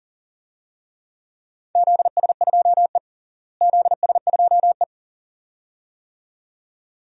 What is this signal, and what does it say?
Morse code, heard as a keyed tone, about 40 wpm, 700 Hz. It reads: ZS2E ZS2E
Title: DXCC-100 Call Signs - 4 Characters - Repeated 2x - 40wpm